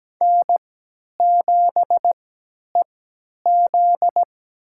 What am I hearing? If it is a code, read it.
Morse code, heard as a keyed tone, about 17 wpm, spaced as for 10 wpm, 700 Hz. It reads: N7EZ
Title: USA Call Signs - 4 Characters - 3x Character Spacing - 17wpm